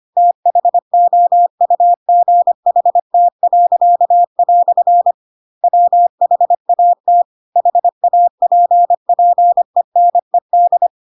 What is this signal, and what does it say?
Morse code, heard as a keyed tone, about 25 wpm, 700 Hz. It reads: THOUGHT." WHAT HAPPENED
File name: Pinocchio / Pinocchio ch3 25 25 ik1hge